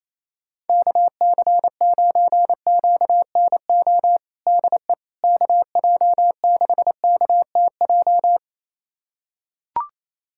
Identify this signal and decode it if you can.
Morse code, heard as a keyed tone, about 28 wpm, 700 Hz. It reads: KC9QNO DE KJ6KTJ
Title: USA Callsign DE Callsign 28wpm